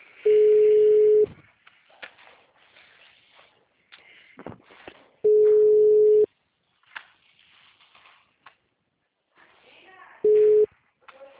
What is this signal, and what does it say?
Ringback tone (Europe, 425 Hz), with speech.
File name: out- -2040- - - .40